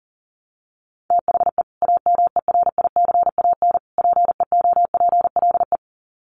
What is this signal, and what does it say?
Morse code, heard as a keyed tone, about 40 wpm, 700 Hz. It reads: THE AMERICAN PEOPLE
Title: Sets of 3 Words 40wpm